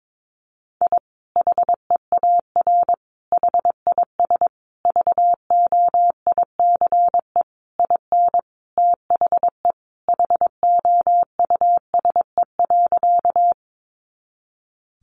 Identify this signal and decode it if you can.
Morse code, heard as a keyed tone, about 22 wpm, 700 Hz. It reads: I HEAR HIS VOICE IN THE HOUSE.
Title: Sentences from Top 500 Words 22wpm